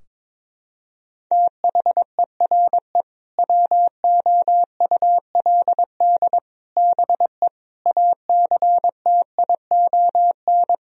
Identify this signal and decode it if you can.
Morse code, heard as a keyed tone, about 22 wpm, 700 Hz. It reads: THERE WOULD BE ACTION
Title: Sentences from Top 1000 Words - Review Part 4 - 22wpm